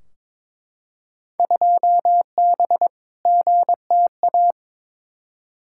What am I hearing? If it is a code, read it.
Morse code, heard as a keyed tone, about 22 wpm, 700 Hz. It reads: 2B GTA
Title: Field Day Exchanges Encore - 22wpm